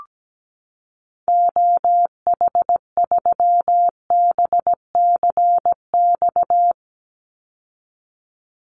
Morse code, heard as a keyed tone, about 17 wpm, 700 Hz. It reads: OH3BCX
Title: Call Signs Repeated 2x - 17wpm